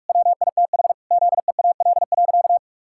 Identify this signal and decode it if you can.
Morse code, heard as a keyed tone, about 47 wpm, 700 Hz. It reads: WITH ZEAL.
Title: Pangram Challenge v2 - 45wpm